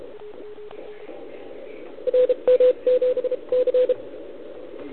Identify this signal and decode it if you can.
Morse code, heard as a keyed tone, about 34 wpm, spaced as for 28 wpm, 500 Hz, with radio noise. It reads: RM7C
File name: ARCHIV / OKDX2020 / OL4Wa